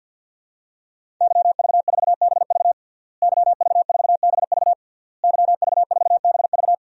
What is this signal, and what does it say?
Morse code, heard as a keyed tone, about 52 wpm, 700 Hz. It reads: YV4BV YV4BV YV4BV
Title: DXCC-100 Call Signs - 5 Characters - Repeated 3x - 50wpm